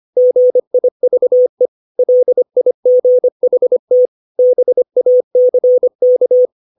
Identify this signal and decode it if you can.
Morse code, heard as a keyed tone, about 25 wpm, 500 Hz. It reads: GIVE LIGHT BACK
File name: 632 words / 632 Words 25WPM